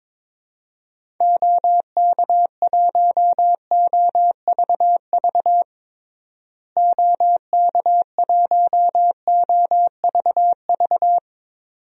Morse code, heard as a keyed tone, about 22 wpm, 700 Hz. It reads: OK1OVV OK1OVV
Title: DXCC-100 Call Signs - 6 Characters - Repeated 2x - 22wpm